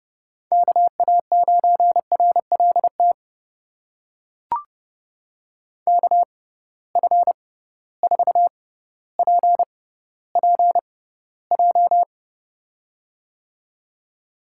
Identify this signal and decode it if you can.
Morse code, heard as a keyed tone, about 30 wpm, spaced as for 15 wpm, 700 Hz. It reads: KA9RLT KF4PPJ
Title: USA Call Signs - 6x Character Spacing - 30wpm